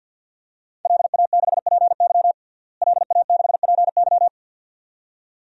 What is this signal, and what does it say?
Morse code, heard as a keyed tone, about 51 wpm, 700 Hz. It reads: LA6PY LA6PY
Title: DXCC-100 Call Signs - 5 Characters - Repeated 2x - 50wpm